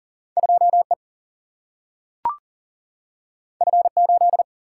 Morse code, heard as a keyed tone, about 40 wpm, 700 Hz. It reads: 2 E F 8
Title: Single-Letter Single-Number 40wpm